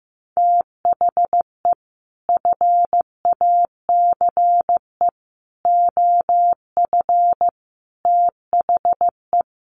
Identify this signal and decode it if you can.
Morse code, heard as a keyed tone, about 15 wpm, 700 Hz. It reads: THE FACE OF THE
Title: Sets of 4 Words 15wpm